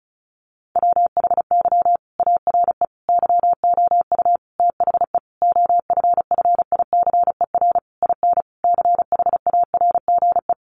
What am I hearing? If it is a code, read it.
Morse code, heard as a keyed tone, about 34 wpm, 700 Hz. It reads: WHY ARE YOU THE OFFICER IN CHARGE
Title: Sentences from Top 1200 Words 35wpm